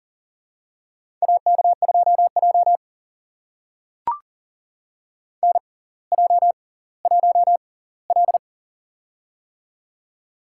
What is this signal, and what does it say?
Morse code, heard as a keyed tone, about 40 wpm, spaced as for 20 wpm, 700 Hz. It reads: AK2J NJ1L
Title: USA Call Signs - 4 Characters - 6x Character Spacing - 40wpm